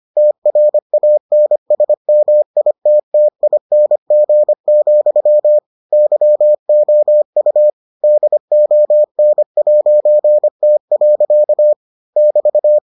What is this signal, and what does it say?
Morse code, heard as a keyed tone, about 25 wpm, 600 Hz. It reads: TRANSMITTING, YOU DON'T. <BT>